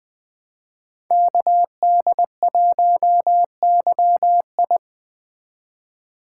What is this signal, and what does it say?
Morse code, heard as a keyed tone, about 20 wpm, 700 Hz. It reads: KD1YI
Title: Speed Racing - USA Call Signs 17wpm